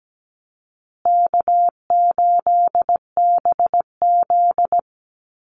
Morse code, heard as a keyed tone, about 17 wpm, 700 Hz. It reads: K8BZ
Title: Easy Call Signs Repeated 3x - 17wpm